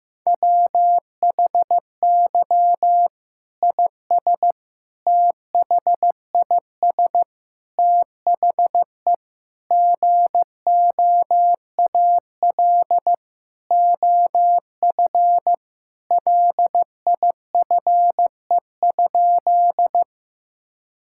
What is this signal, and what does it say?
Morse code, heard as a keyed tone, about 15 wpm, 700 Hz. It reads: WHY IS THIS THE GOAL OF LIFE?
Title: Sentences from Top 1000 Words - Review Part 5 - 15wpm